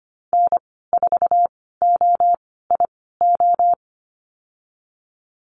Morse code, heard as a keyed tone, about 25 wpm, spaced as for 16 wpm, 700 Hz. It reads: N4OIO
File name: Lesson 15g - Number 5 - Call Signs 25wpm